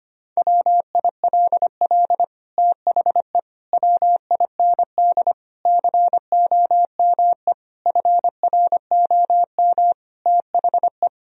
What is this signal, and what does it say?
Morse code, heard as a keyed tone, about 25 wpm, 700 Hz. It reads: WILL THE WIND COME FROM THE
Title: Sentences from Top 1000 Words - Review Part 2 - 25wpm